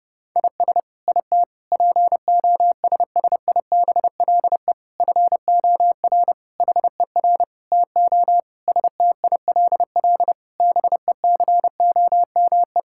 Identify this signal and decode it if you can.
Morse code, heard as a keyed tone, about 28 wpm, 700 Hz. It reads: IS IT POSSIBLE FOR HER TO STILL BECOME
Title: Sentences from Top 1000 Words - Review Part 6 - 30wpm